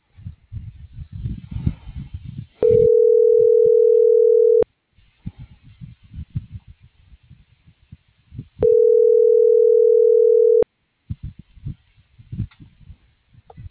Ringback tone (North America), 440 + 480 Hz.